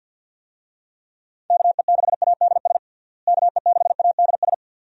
Morse code, heard as a keyed tone, about 51 wpm, 700 Hz. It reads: KE6ADS KE6ADS
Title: USA Call Signs - 6 Characters - Repeated 2x - 50wpm